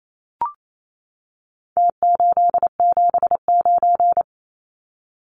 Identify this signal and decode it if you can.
Morse code, heard as a keyed tone, about 28 wpm, 700 Hz. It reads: T879
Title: Numbers - Contest Style - 4-Digit 28wpm